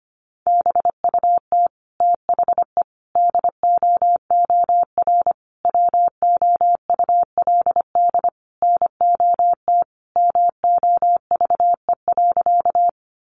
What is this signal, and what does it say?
Morse code, heard as a keyed tone, about 25 wpm, 700 Hz. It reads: BUT THE DOOR WOULD NOT MOVE.
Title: Sentences from Top 600 Words 25wpm